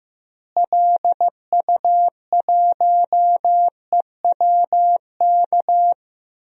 Morse code, heard as a keyed tone, about 15 wpm, 700 Hz. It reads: LU1EWK
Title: DXCC-100 Call Signs - 6 Characters - Repeated 3x - 15wpm